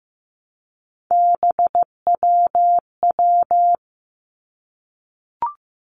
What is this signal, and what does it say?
Morse code, heard as a keyed tone, about 15 wpm, 700 Hz. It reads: BWW
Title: Three Challenging Letter-Number Combinations 15wpm